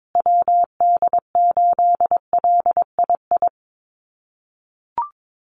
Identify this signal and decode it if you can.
Morse code, heard as a keyed tone, about 22 wpm, 700 Hz. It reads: WD8LII
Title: USA Call Signs Repeated 2x 22wpm